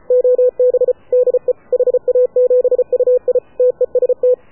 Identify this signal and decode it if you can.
Morse code, heard as a keyed tone, about 33 wpm, 500 Hz. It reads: OB DE HA7UI TEST
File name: AUDIO2019 / HA7UI CW